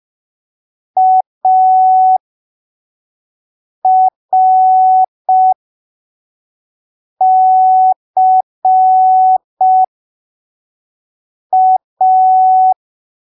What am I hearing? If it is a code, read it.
Morse code, heard as a keyed tone, about 5 wpm, spaced as for 3 wpm, 750 Hz. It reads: ARCA
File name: arquivos / texto / texto 02 5ppm